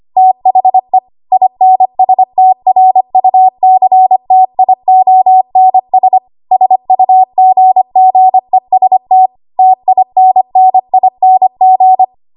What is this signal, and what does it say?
Morse code, heard as a keyed tone, about 25 wpm, 750 Hz. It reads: THE INSTRUCTIONS SUGGEST TINNING
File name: Archive / 25 WPM / 25WPM